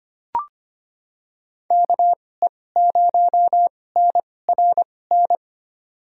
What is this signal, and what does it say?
Morse code, heard as a keyed tone, about 25 wpm, spaced as for 18 wpm, 700 Hz. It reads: KE0NRN
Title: USA Call Signs - 6 Characters - 2x Character Spacing - 25wpm